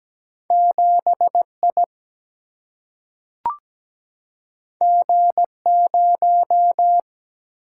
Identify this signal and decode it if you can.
Morse code, heard as a keyed tone, about 17 wpm, 700 Hz. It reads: 7I G0
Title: Single-Letter Single-Number 17wpm